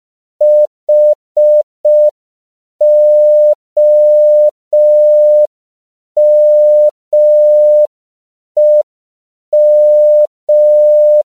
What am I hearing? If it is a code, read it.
Morse code, heard as a keyed tone, about 5 wpm, 600 Hz. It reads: HOMEM
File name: CW / 5PPM